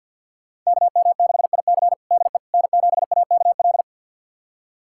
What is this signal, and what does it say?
Morse code, heard as a keyed tone, about 52 wpm, 700 Hz. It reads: KM6IC DE N7AKL
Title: USA Callsign DE Callsign - 5 Characters - 50wpm